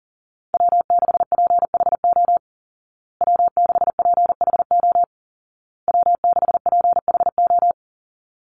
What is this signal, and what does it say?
Morse code, heard as a keyed tone, about 40 wpm, 700 Hz. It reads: W6PHO W6PHO W6PHO
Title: Call Signs Repeated 2x - 40wpm